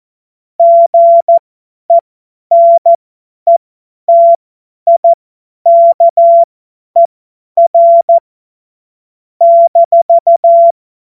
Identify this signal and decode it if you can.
Morse code, heard as a keyed tone, about 14 wpm, spaced as for 10 wpm, 680 Hz. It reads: GENETIKER -
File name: Morsekurs / Witze / Witz 28.November 2023 50 F70